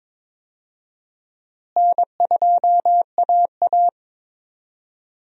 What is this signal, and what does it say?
Morse code, heard as a keyed tone, about 22 wpm, 700 Hz. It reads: N2AA
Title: USA Call Signs - 4 Characters - Repeated 3x - 22wpm